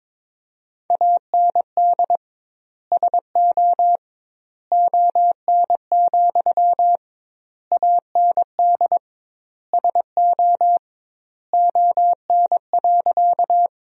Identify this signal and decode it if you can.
Morse code, heard as a keyed tone, about 21 wpm, 700 Hz. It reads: AND SO ON, AND SO ON.
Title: 2x Word-Spacing - Sentences from Top 100 Words - 22wpm